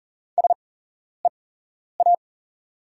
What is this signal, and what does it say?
Morse code, heard as a keyed tone, about 40 wpm, spaced as for 11 wpm, 690 Hz. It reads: SEA
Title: 8x Character Spacing - Top 300 Words - 40wpm